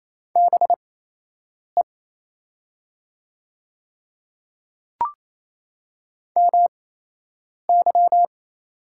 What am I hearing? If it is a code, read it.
Morse code, heard as a keyed tone, about 28 wpm, spaced as for 8 wpm, 700 Hz. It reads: BE MY
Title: 8x Character Spacing - Top 100 Words - 28wpm